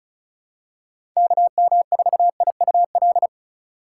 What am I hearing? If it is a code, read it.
Morse code, heard as a keyed tone, about 34 wpm, 700 Hz. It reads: KM4IUL
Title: USA Call Signs - 6 Characters - 2x Character Spacing - 35wpm